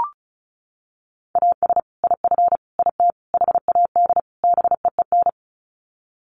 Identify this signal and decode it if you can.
Morse code, heard as a keyed tone, about 34 wpm, 700 Hz. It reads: AS IF IT HAD BEEN
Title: Sets of 5 Words Encore2 35wpm